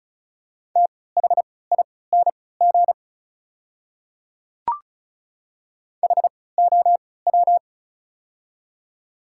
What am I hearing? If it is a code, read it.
Morse code, heard as a keyed tone, about 34 wpm, spaced as for 20 wpm, 690 Hz. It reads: THING HOW
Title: 3x Character Spacing - Top 300 Words - 35wpm